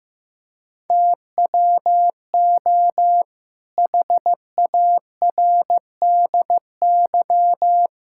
Morse code, heard as a keyed tone, about 15 wpm, 700 Hz. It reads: TWO HARDY
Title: Pangram Challenge v2 - 15wpm